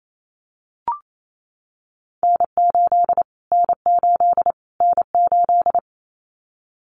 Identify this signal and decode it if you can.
Morse code, heard as a keyed tone, about 28 wpm, 700 Hz. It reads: N8 N8 N8
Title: Callsign Prefixes for Top 25 DXCC - Repeated 3x - 28wpm